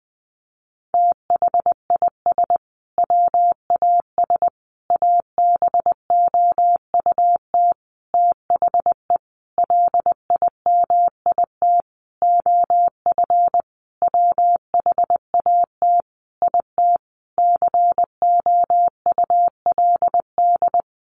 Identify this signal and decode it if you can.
Morse code, heard as a keyed tone, about 20 wpm, 700 Hz. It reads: THIS WAS ABOUT THE LIMIT OF WHAT IT COULD
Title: Sentences from Top 1500 Words 20wpm